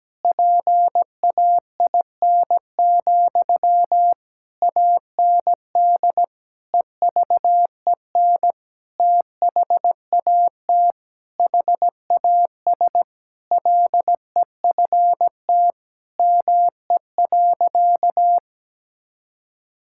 Morse code, heard as a keyed tone, about 17 wpm, 700 Hz. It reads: PAIN, AND EVEN THAT HAS LEFT ME.
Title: Sentences from Top 1600 Words 17wpm